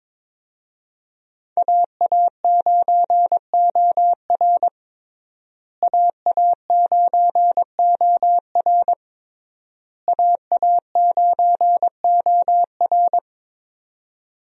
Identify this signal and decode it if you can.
Morse code, heard as a keyed tone, about 22 wpm, 700 Hz. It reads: AA9OR AA9OR AA9OR
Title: USA Call Signs - 5 Characters - Repeated 3x - 22wpm